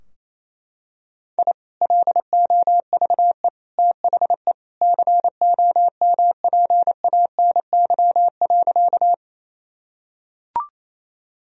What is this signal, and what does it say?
Morse code, heard as a keyed tone, about 28 wpm, 700 Hz. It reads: I LOVE THE COMPANY.
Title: Sentences from Top 1000 Words - Review Part 3 - 28wpm